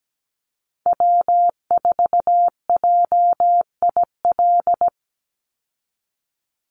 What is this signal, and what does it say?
Morse code, heard as a keyed tone, about 17 wpm, 700 Hz. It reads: W4JIL
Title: USA Call Signs 17wpm